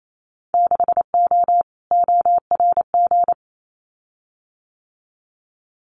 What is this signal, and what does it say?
Morse code, heard as a keyed tone, about 28 wpm, 700 Hz. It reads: 6O ORG
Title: Winter Field Day Exchanges Encore 28wpm